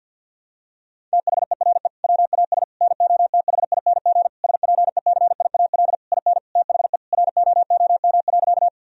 Morse code, heard as a keyed tone, about 52 wpm, 700 Hz. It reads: THERE WAS NOTHING SPECIAL IN THE ROOM.
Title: Sentences from Top 1000 Words - Review Part 3 - 50wpm